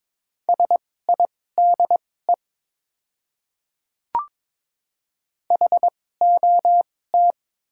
Morse code, heard as a keyed tone, about 22 wpm, spaced as for 16 wpm, 700 Hz. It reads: SIDE HOT
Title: 2x Character Spacing - Top 100 Words - 22wpm